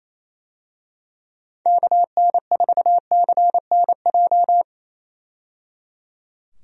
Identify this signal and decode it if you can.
Morse code, heard as a keyed tone, about 28 wpm, 700 Hz. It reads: KN4CNJ
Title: USA Call Signs - 6 Characters - 28wpm